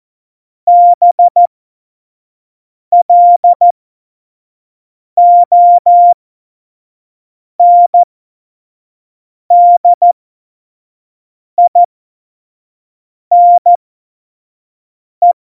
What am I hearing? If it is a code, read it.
Morse code, heard as a keyed tone, about 14 wpm, spaced as for 5 wpm, 700 Hz. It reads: BLONDINE
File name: Morsekurs / Witze / Witz 04 Juli 17, Tempo 25 F70